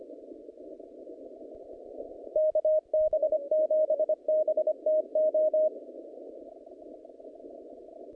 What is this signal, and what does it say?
Morse code, heard as a keyed tone, about 25 wpm, 620 Hz. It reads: KB7BTO
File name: contests / naqp-cw-k5tr-op-k5pi- -0100